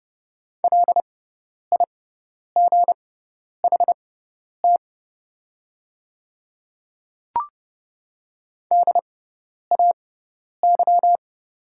Morse code, heard as a keyed tone, about 30 wpm, spaced as for 10 wpm, 700 Hz. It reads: LIGHT DAY
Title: 6x Character Spacing - Top 200 Words - 30wpm